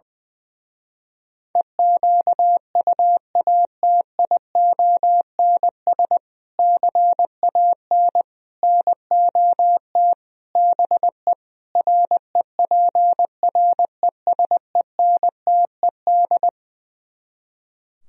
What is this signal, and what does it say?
Morse code, heard as a keyed tone, about 20 wpm, 700 Hz. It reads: EQUATIONS CAN NOT BE REPRESENTED
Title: Speed-Racing - Sets of 5 Words 20wpm